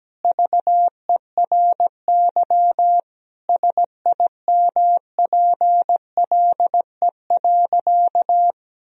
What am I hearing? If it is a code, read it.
Morse code, heard as a keyed tone, about 17 wpm, 700 Hz. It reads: VERY SIMPLE.
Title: Sentences from Top 1000 Words - Review Part 4 - 17wpm